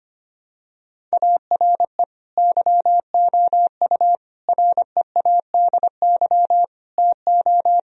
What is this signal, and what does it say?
Morse code, heard as a keyed tone, about 25 wpm, 700 Hz. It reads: ARE YOU READY TO
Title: Sentences from Top 1000 Words - Review Part 1 - 25wpm